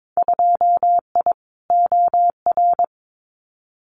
Morse code, heard as a keyed tone, about 22 wpm, 700 Hz. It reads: 2I OR
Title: Winter Field Day Exchanges Encore 22wpm